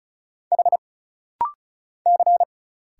Morse code, heard as a keyed tone, about 35 wpm, 700 Hz. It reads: H C
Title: Single Letters - Rapid-Fire 35wpm